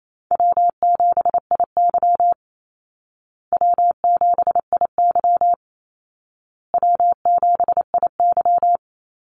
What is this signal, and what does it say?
Morse code, heard as a keyed tone, about 28 wpm, 700 Hz. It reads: W7IY W7IY W7IY
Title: Call Signs Repeated 2x - 28wpm